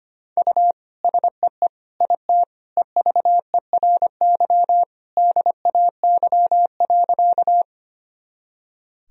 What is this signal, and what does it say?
Morse code, heard as a keyed tone, about 25 wpm, 700 Hz. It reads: U SEE IT EVERY DAY.
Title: Sentences from Top 200 Words and Abbreviations 25wpm